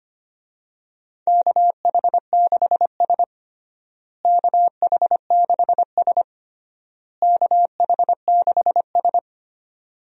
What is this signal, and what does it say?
Morse code, heard as a keyed tone, about 25 wpm, 700 Hz. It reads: KH6S KH6S KH6S
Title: DXCC-100 Call Signs - 4 Characters - Repeated 3x - 25wpm